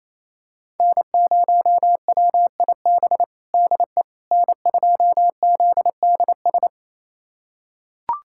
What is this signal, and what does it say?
Morse code, heard as a keyed tone, about 28 wpm, 700 Hz. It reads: N0WIB DE N2ZDS
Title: USA Callsign DE Callsign - 5 Characters - 28wpm